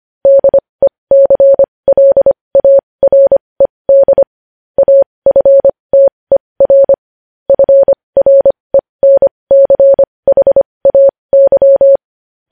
Morse code, heard as a keyed tone, about 25 wpm, spaced as for 20 wpm, 550 Hz. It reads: DECLARED AFTER FRENCHAY
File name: podcast / News headlines-1x-25x20